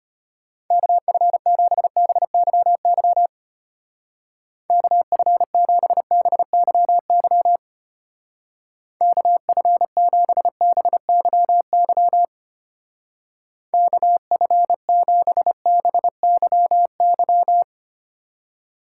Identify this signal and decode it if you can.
Morse code, heard as a keyed tone, about 32 wpm, 700 Hz. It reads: KF7BYY KF7BYY KF7BYY KF7BYY